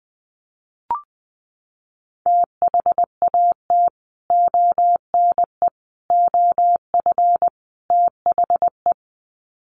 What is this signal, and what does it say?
Morse code, heard as a keyed tone, about 20 wpm, 700 Hz. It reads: THAT ONE OF THE
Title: Sets of 4 Words 20wpm